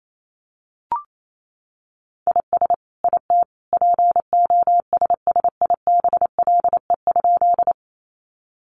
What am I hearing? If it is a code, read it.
Morse code, heard as a keyed tone, about 28 wpm, 700 Hz. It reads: IS IT POSSIBLE?
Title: Sentences from Top 500 Words 28wpm